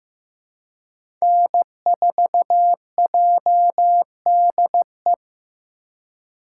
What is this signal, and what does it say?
Morse code, heard as a keyed tone, about 15 wpm, 700 Hz. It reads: N4JDE
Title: USA Call Signs - 5 Characters - 2x Character Spacing - 15wpm